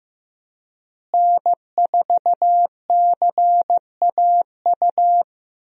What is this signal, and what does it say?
Morse code, heard as a keyed tone, about 15 wpm, 700 Hz. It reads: N4CAU
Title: USA Call Signs - 5 Characters - Repeated 2x - 15wpm